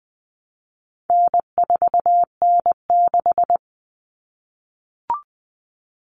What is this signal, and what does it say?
Morse code, heard as a keyed tone, about 20 wpm, 700 Hz. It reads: N4N6
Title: Common Cut Numbers - 4-Digit 20wpm